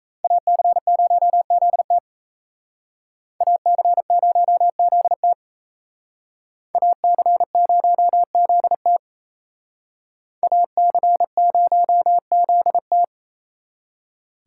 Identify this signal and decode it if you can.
Morse code, heard as a keyed tone, about 34 wpm, 700 Hz. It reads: AC0ZT AC0ZT AC0ZT AC0ZT
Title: Speed Racing - USA 2x2 Call Signs 28wpm